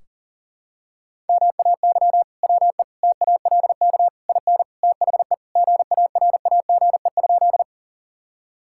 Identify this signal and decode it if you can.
Morse code, heard as a keyed tone, about 40 wpm, 700 Hz. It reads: MAY WE TALK IN THE GARAGE?
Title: Sentences from Top 1600 Words 40wpm